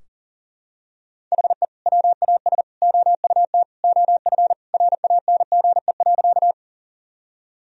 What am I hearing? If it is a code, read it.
Morse code, heard as a keyed tone, about 40 wpm, 700 Hz. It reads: HE WAS OUT OF RANGE.
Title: Sentences from Top 1000 Words - Review Part 4 - 40wpm